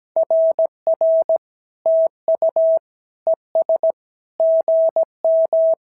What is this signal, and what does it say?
Morse code, heard as a keyed tone, about 17 wpm, 650 Hz. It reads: RR TU ES GM
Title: SOTA QSO Practice 17wpm